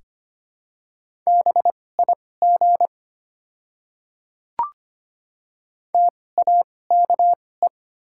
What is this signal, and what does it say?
Morse code, heard as a keyed tone, about 25 wpm, spaced as for 18 wpm, 690 Hz. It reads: BIG TAKE
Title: 2x Character Spacing - Top 300 Words - 25wpm